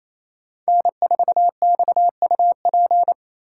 Morse code, heard as a keyed tone, about 34 wpm, 700 Hz. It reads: N4XUP
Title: Speed Racing - USA 1x3 Call Signs 28wpm